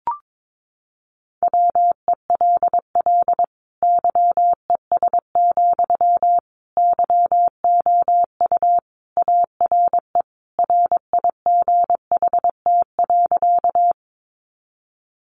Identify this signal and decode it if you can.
Morse code, heard as a keyed tone, about 22 wpm, 700 Hz. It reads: WELL YES, YOU ARE RIGHT.
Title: Sentences from Top 500 Words 22wpm